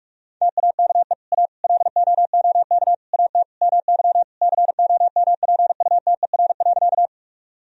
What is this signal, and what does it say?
Morse code, heard as a keyed tone, about 46 wpm, 700 Hz. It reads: TAKE A LOOK AT MY COMPUTER.
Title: Sentences from Top 1000 Words - Review Part 3 - 45wpm